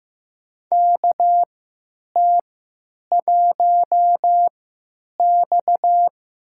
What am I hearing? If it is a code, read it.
Morse code, heard as a keyed tone, about 15 wpm, spaced as for 9 wpm, 700 Hz. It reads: KT1X
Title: USA Call Signs - 4 Characters - 3x Character Spacing - 15wpm